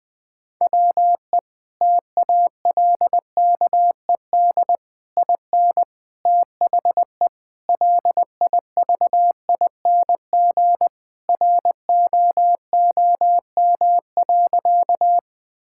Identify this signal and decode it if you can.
Morse code, heard as a keyed tone, about 20 wpm, 700 Hz. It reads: WE TALKED IN THE LIVING ROOM.
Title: Sentences with 4-Word Phrases - 20wpm